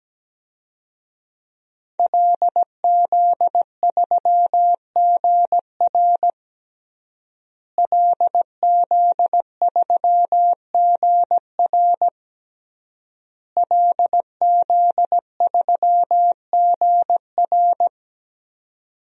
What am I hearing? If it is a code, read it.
Morse code, heard as a keyed tone, about 17 wpm, 700 Hz. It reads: LZ3GR LZ3GR LZ3GR
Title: DXCC-100 Call Signs - 5 Characters - Repeated 3x - 17wpm